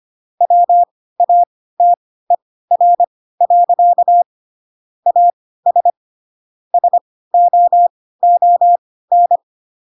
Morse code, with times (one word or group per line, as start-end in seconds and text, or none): 0.4-4.2 WATER.
5.1-5.9 AS
6.7-9.4 SOON